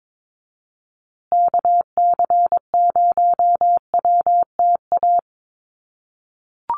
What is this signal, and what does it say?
Morse code, heard as a keyed tone, about 22 wpm, 700 Hz. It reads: KC0WTA
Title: USA Call Signs Repeated 3x 22wpm